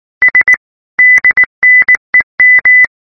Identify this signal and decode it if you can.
Morse code (Russian alphabet), heard as a keyed tone, about 19 wpm, 2000 Hz. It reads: С ДНЕМ